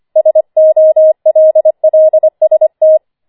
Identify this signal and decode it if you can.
Morse code, heard as a keyed tone, about 25 wpm, 630 Hz. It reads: SOLLST
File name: 25wpm